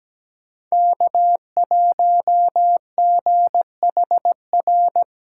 Morse code, heard as a keyed tone, about 17 wpm, 700 Hz. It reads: K1GHR
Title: USA Call Signs - 5 Characters - 4x Character Spacing - 17wpm